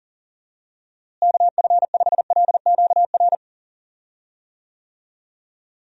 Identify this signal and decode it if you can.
Morse code, heard as a keyed tone, about 40 wpm, 700 Hz. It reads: KF5LQR
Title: USA Call Signs - 6 Characters - Repeated 3x - 40wpm